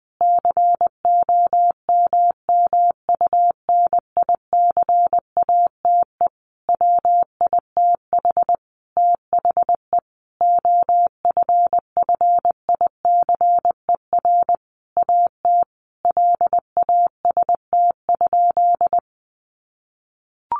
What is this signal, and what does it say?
Morse code, heard as a keyed tone, about 19 wpm, 700 Hz. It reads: COMMUNICATE WITH THE OFFICER AT LAST?
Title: Sentences from Top 1300 Words 20wpm